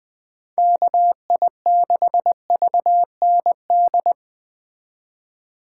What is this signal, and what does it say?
Morse code, heard as a keyed tone, about 21 wpm, 700 Hz. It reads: KI6VND